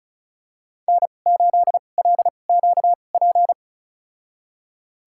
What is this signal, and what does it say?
Morse code, heard as a keyed tone, about 34 wpm, spaced as for 25 wpm, 690 Hz. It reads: N8LQP
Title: USA Call Signs - 5 Characters - 2x Character Spacing - 35wpm